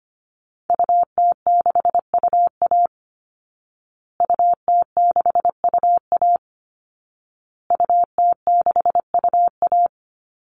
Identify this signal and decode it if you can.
Morse code, heard as a keyed tone, about 25 wpm, 700 Hz. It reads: UT6UA UT6UA UT6UA
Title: Call Signs Repeated 2x - 25wpm